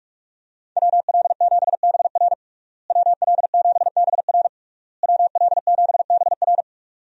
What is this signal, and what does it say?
Morse code, heard as a keyed tone, about 42 wpm, 700 Hz. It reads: WL7BR WL7BR WL7BR
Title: USA Call Signs - 5 Characters - Repeated 3x - 45wpm